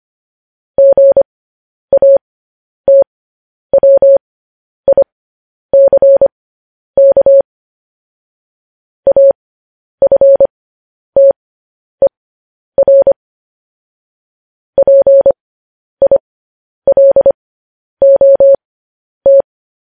Morse code, heard as a keyed tone, about 25 wpm, spaced as for 10 wpm, 550 Hz. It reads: GATWICK AFTER PILOT
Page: news-headlines-in-morse-code-at-10-wpm